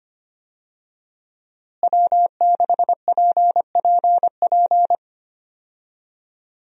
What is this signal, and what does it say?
Morse code, heard as a keyed tone, about 25 wpm, 700 Hz. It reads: W6PPP
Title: USA Call Signs - 5 Characters - Repeated 2x - 25wpm